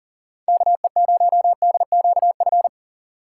Morse code, heard as a keyed tone, about 40 wpm, 700 Hz. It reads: KE0DQF
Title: USA Call Signs - 6 Characters - 3x Character Spacing - 40wpm